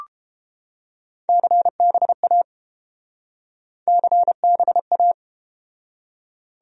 Morse code, heard as a keyed tone, about 31 wpm, 700 Hz. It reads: CBA CBA